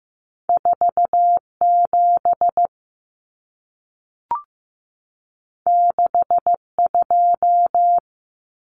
Morse code, heard as a keyed tone, about 15 wpm, 700 Hz. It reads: 47 62
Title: Numbers - 2-Digit 15wpm